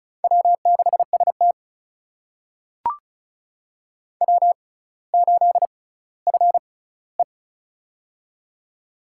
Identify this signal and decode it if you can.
Morse code, heard as a keyed tone, about 36 wpm, spaced as for 18 wpm, 700 Hz. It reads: W6ST W8FE
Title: USA Call Signs - 4 Characters - 6x Character Spacing - 35wpm